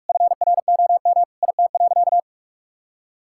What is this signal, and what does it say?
Morse code, heard as a keyed tone, about 46 wpm, 700 Hz. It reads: FROM IT.
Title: Sentences from Top 1000 Words - Review Part 5 - 45wpm